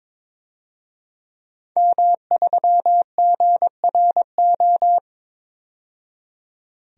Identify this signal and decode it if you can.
Morse code, heard as a keyed tone, about 22 wpm, 700 Hz. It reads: M3GRO
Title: DXCC-100 Call Signs - 5 Characters - 22wpm